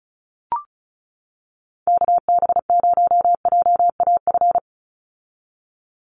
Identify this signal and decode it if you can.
Morse code, heard as a keyed tone, about 34 wpm, 700 Hz. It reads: KB0JAF
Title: USA Call Signs 35wpm